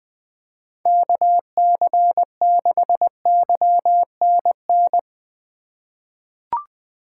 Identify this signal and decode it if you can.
Morse code, heard as a keyed tone, about 20 wpm, 700 Hz. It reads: KC6YNN
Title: USA Call Signs - 6 Characters - 4x Character Spacing - 20wpm